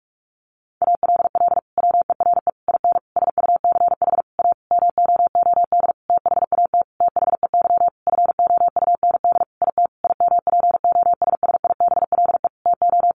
Morse code, heard as a keyed tone, about 44 wpm, 700 Hz. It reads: ALL WERE IN SUCH A MOOD THAT THEY FOUND IT IMPOSSIBLE TO
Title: Sentences from Top 1100 Words 45wpm